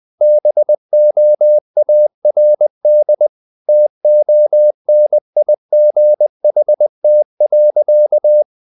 Morse code, heard as a keyed tone, about 20 wpm, 600 Hz. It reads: BOARD TONIGHT.
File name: the old man and the sea / the old man and the sea part 1 20wpm